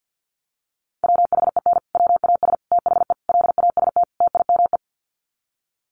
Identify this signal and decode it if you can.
Morse code, heard as a keyed tone, about 48 wpm, 700 Hz. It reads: WHEN WAS THE LAST TIME
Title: Sets of 5 Words Encore 50wpm